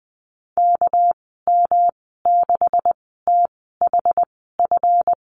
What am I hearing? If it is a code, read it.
Morse code, heard as a keyed tone, about 20 wpm, spaced as for 14 wpm, 700 Hz. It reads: KM6THF
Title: 213 Lesson 34g - Number 8 - Call Signs 20wpm